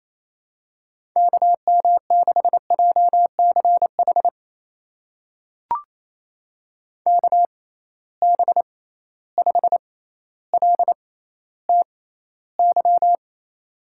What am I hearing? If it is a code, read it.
Morse code, heard as a keyed tone, about 28 wpm, spaced as for 14 wpm, 700 Hz. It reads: KM6JCH KB5LTY
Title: USA Call Signs - 6 Characters - 6x Character Spacing - 28wpm